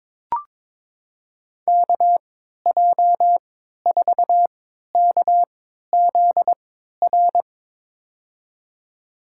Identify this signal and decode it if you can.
Morse code, heard as a keyed tone, about 22 wpm, spaced as for 13 wpm, 700 Hz. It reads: KJ4KZR
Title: USA Call Signs - 6 Characters - 3x Character Spacing - 22wpm